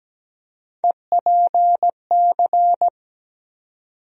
Morse code, heard as a keyed tone, about 17 wpm, 700 Hz. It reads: EPC
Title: USA Call Sign Postfixes - 3 Letters 17wpm